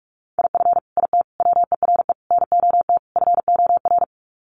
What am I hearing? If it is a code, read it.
Morse code, heard as a keyed tone, about 44 wpm, 700 Hz. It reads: IF IT WERE NOT FOR
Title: Sets of 5 Words 45wpm